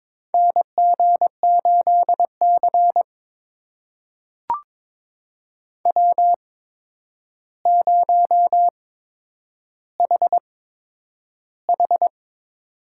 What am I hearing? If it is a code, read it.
Morse code, heard as a keyed tone, about 22 wpm, spaced as for 9 wpm, 700 Hz. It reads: NG8C W0HH
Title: USA Call Signs - 4 Characters - 8x Character Spacing - 22wpm